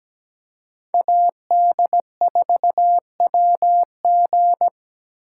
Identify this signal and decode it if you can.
Morse code, heard as a keyed tone, about 20 wpm, 700 Hz. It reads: AD4WG
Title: Speed Racing - USA 2x2 Call Signs 17wpm